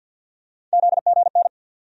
Morse code, heard as a keyed tone, about 53 wpm, 700 Hz. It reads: ZGN
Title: USA Call Sign Postfixes - 3 Letters 50wpm